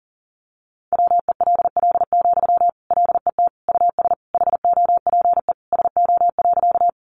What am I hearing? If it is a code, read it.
Morse code, heard as a keyed tone, about 40 wpm, 700 Hz. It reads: WELL, LET US HOPE SO.
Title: Sentences from Top 1000 Words 40wpm